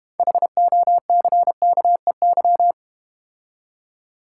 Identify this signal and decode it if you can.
Morse code, heard as a keyed tone, about 31 wpm, 700 Hz. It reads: HOCKEY
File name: Easy Words - Speed-Racing with Hints - 25wpm